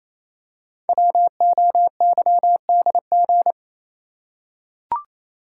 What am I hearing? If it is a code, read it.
Morse code, heard as a keyed tone, about 27 wpm, 700 Hz. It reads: WOYDG
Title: Code Groups - 5 Letters - 28wpm